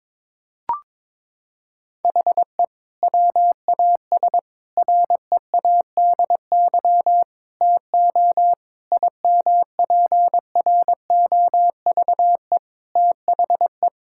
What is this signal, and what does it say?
Morse code, heard as a keyed tone, about 22 wpm, 700 Hz. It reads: HE WAS READY TO IMPROVE THE
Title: Sentences from Top 1000 Words - Review Part 5 - 22wpm